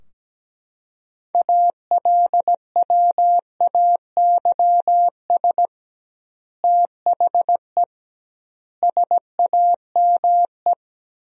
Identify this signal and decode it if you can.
Morse code, heard as a keyed tone, about 17 wpm, 700 Hz. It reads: ALWAYS THE SAME
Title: 2x Word-Spacing - Sentences from Top 300 Words - 17wpm